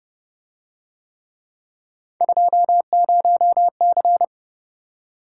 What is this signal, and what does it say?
Morse code, heard as a keyed tone, about 29 wpm, 700 Hz. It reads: 20C